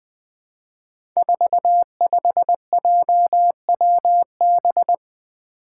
Morse code, heard as a keyed tone, about 20 wpm, 700 Hz. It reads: 45JWB
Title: Five Challenging Letter-Number Combinations 20wpm